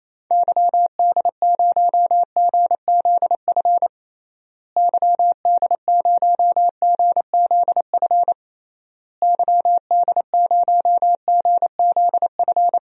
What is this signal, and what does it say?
Morse code, heard as a keyed tone, about 28 wpm, 700 Hz. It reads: YD0GZF YD0GZF YD0GZF
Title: DXCC-100 Call Signs - 6 Characters - Repeated 3x - 28wpm